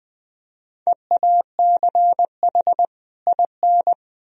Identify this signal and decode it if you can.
Morse code, heard as a keyed tone, about 20 wpm, 700 Hz. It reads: EACH IN
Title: Sentences from Top 1000 Words - Review Part 1 - 20wpm